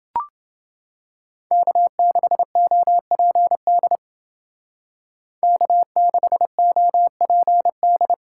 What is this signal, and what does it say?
Morse code, heard as a keyed tone, about 29 wpm, 700 Hz. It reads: K6OPD K6OPD